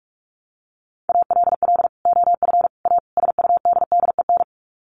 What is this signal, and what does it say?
Morse code, heard as a keyed tone, about 44 wpm, 700 Hz. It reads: ALL OF A SUDDEN
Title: Sets of 4 Words 45wpm